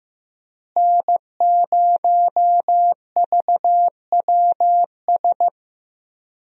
Morse code, heard as a keyed tone, about 15 wpm, 700 Hz. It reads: N0VWS
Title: USA Call Signs - 5 Characters - Repeated 3x - 15wpm